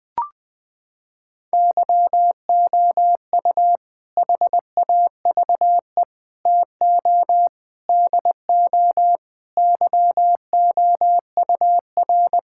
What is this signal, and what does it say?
Morse code, heard as a keyed tone, about 21 wpm, 700 Hz. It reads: YOU HAVE TO DO YOUR
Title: Sentences with 4-Word Phrases - 20wpm